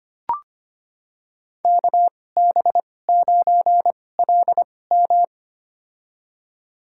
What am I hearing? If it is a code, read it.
Morse code, heard as a keyed tone, about 25 wpm, spaced as for 18 wpm, 700 Hz. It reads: KB9LM
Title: USA Call Signs - 5 Characters - 2x Character Spacing - 25wpm